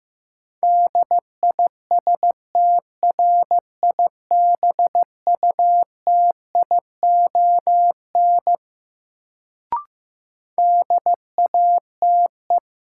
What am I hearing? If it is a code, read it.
Morse code, heard as a keyed tone, about 15 wpm, 700 Hz. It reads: DISTRIBUTION DATE
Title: Sentences from Top 1000 Words - Review Part 7 - New Word Drill - 15wpm